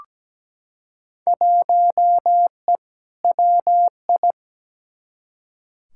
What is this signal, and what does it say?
Morse code, heard as a keyed tone, about 17 wpm, 700 Hz. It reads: 1E WI
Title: Field Day Exchanges - 17wpm